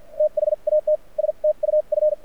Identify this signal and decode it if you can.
Morse code, heard as a keyed tone, about 50 wpm, spaced as for 41 wpm, 590 Hz. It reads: THAT STUF
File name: archived / CRSnet